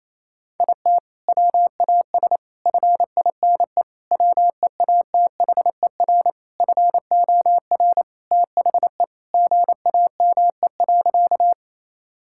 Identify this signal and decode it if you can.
Morse code, heard as a keyed tone, about 28 wpm, 700 Hz. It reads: IT WAS FINE WEATHER FOR THE GAME.
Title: Sentences from Top 1000 Words - Review Part 5 - 28wpm